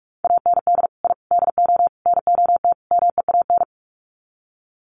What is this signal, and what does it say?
Morse code, heard as a keyed tone, about 44 wpm, 700 Hz. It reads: AND I DO NOT MEAN
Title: Sets of 5 Words Encore2 45wpm